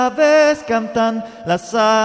There is no code.